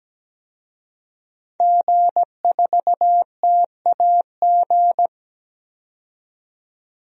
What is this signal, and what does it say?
Morse code, heard as a keyed tone, about 17 wpm, 700 Hz. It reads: G4TAG
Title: DXCC-100 Call Signs - 5 Characters - 17wpm